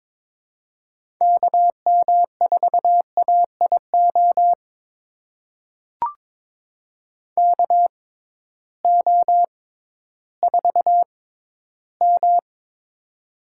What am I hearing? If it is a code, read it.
Morse code, heard as a keyed tone, about 22 wpm, spaced as for 13 wpm, 700 Hz. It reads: KM4AIO KO4M
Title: USA Call Signs - 6x Character Spacing - 22wpm